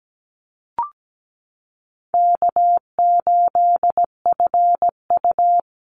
Morse code, heard as a keyed tone, about 17 wpm, 700 Hz. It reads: K8FU
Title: Easy Call Signs Repeated 3x - 17wpm